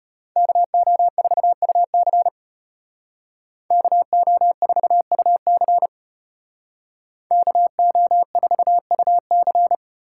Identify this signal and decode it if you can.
Morse code, heard as a keyed tone, about 33 wpm, 700 Hz. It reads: KO4UC KO4UC KO4UC